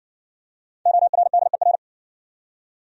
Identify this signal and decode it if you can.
Morse code, heard as a keyed tone, about 60 wpm, 700 Hz. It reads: ORDER